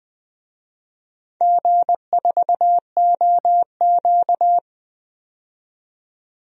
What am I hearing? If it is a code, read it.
Morse code, heard as a keyed tone, about 20 wpm, 700 Hz. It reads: G4OQ
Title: DXCC-100 Call Signs - 4 Characters - 20wpm